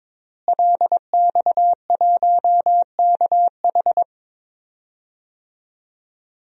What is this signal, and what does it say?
Morse code, heard as a keyed tone, about 22 wpm, 700 Hz. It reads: LX1KH
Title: DXCC-100 Call Signs - 5 Characters - Repeated 3x - 22wpm